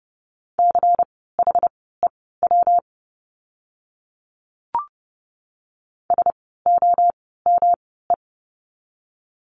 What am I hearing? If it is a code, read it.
Morse code, heard as a keyed tone, about 30 wpm, spaced as for 17 wpm, 700 Hz. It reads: CHEW SOME
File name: course / 30:10wpm / 115 Lesson 19f - Letter w - Words 30wpm